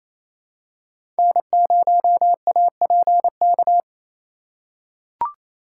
Morse code, heard as a keyed tone, about 28 wpm, 700 Hz. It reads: N0APK
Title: USA Call Signs - 5 Characters - 2x Character Spacing - 28wpm